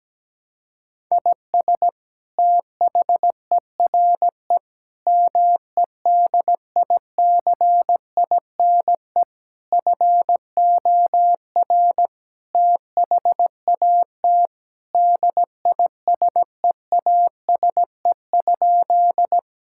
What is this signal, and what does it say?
Morse code, heard as a keyed tone, about 17 wpm, 700 Hz. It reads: IS THERE MEDICINE FOR THAT DISEASE?
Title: Sentences from Top 1000 Words - Review Part 6 - 17wpm